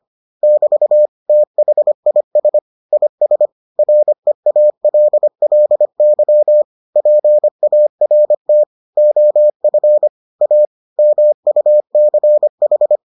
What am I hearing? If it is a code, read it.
Morse code, heard as a keyed tone, about 25 wpm, 600 Hz. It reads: <BT> THIS IS REALLY PART OF A MUCH